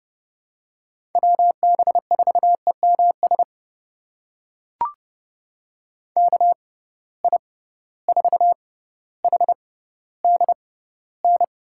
Morse code, heard as a keyed tone, about 32 wpm, spaced as for 16 wpm, 700 Hz. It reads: WB4EMS KI4HDN
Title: USA Call Signs - 6 Characters - 6x Character Spacing - 30wpm